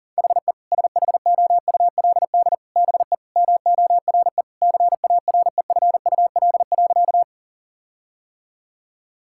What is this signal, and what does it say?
Morse code, heard as a keyed tone, about 37 wpm, 700 Hz. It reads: HE SHOULD BE MORE CAREFUL.
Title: Sentences from Top 1600 Words 40wpm